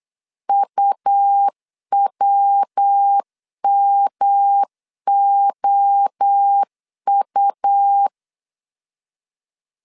Morse code, heard as a keyed tone, about 8 wpm, 800 Hz. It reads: UWMOU
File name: CW / DET / 1 9 1500C S B 8wpm 5624